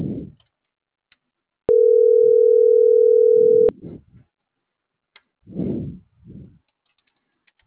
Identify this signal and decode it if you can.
Ringback tone (North America), 440 + 480 Hz.